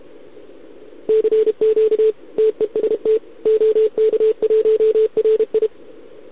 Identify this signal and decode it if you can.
Morse code, heard as a keyed tone, about 31 wpm, 440 Hz. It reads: CQ TEST OK1RI
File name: ARCHIV / OKDX2020 / OK1RIb